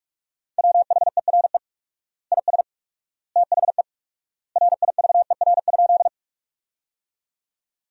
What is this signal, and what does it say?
Morse code, heard as a keyed tone, about 46 wpm, 700 Hz. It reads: WHERE IS THE RIVER?
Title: 4x Word-Spacing - Sentences from Top 300 Words - 45wpm